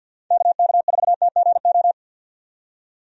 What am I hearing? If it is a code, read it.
Morse code, heard as a keyed tone, about 50 wpm, 700 Hz. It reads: KK4TGO
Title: USA Call Signs - 6 Characters - 50wpm